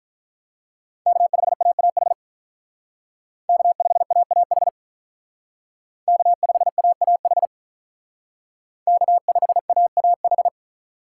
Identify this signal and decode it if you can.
Morse code, heard as a keyed tone, about 43 wpm, 700 Hz. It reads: K5AAH K5AAH K5AAH K5AAH
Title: Speed Racing - USA 1x3 Call Signs 35wpm